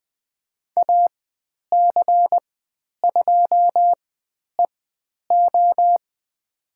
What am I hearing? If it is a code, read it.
Morse code, heard as a keyed tone, about 20 wpm, spaced as for 10 wpm, 700 Hz. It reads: AC2EO
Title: 109 Lesson 18g - Letter m - Call Signs 20wpm